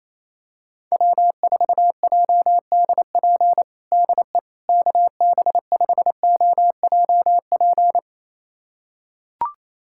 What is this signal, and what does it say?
Morse code, heard as a keyed tone, about 28 wpm, 700 Hz. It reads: W4JDP DE KB5OJP
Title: USA Callsign DE Callsign 28wpm